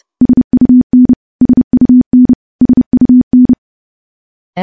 Morse code, heard as a keyed tone, about 30 wpm, 250 Hz. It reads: SUN SUN SUN
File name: morse / morse-training-30WPM